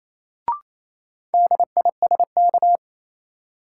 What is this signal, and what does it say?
Morse code, heard as a keyed tone, about 28 wpm, 700 Hz. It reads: DISK
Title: Sentences from Top 1000 Words - Review Part 6 - New Word Drill - 28wpm